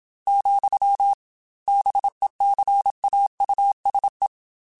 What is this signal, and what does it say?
Morse code, heard as a keyed tone, about 27 wpm, 800 Hz. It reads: , BECAUSE